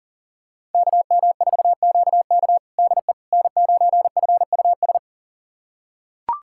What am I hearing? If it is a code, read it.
Morse code, heard as a keyed tone, about 40 wpm, 700 Hz. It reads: KM4QK DE N9FUS
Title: USA Callsign DE Callsign - 5 Characters - 40wpm